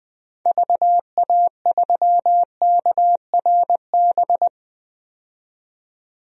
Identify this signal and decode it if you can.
Morse code, heard as a keyed tone, about 20 wpm, 700 Hz. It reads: VA3KRB
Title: DXCC-100 Call Signs - 6 Characters - 20wpm